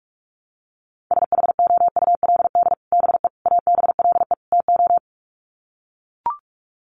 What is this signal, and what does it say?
Morse code, heard as a keyed tone, about 44 wpm, 700 Hz. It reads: SHOULD BE ABLE TO
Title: Sets of 4 Words 45wpm